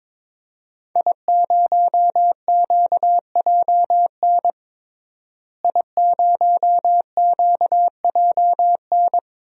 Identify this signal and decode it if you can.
Morse code, heard as a keyed tone, about 22 wpm, 700 Hz. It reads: I0QJN I0QJN
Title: DXCC-100 Call Signs - 5 Characters - Repeated 3x - 22wpm